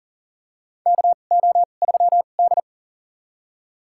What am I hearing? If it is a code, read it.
Morse code, heard as a keyed tone, about 40 wpm, spaced as for 29 wpm, 700 Hz. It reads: KO3D
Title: USA Call Signs - 4 Characters - 2x Character Spacing - 40wpm